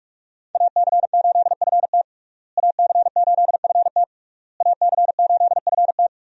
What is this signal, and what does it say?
Morse code, heard as a keyed tone, about 45 wpm, 700 Hz. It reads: AC8FT AC8FT AC8FT
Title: USA Call Signs - 5 Characters - Repeated 3x - 45wpm